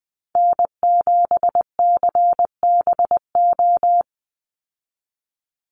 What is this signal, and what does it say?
Morse code, heard as a keyed tone, about 20 wpm, 700 Hz. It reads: N7CBO
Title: USA Call Signs 20wpm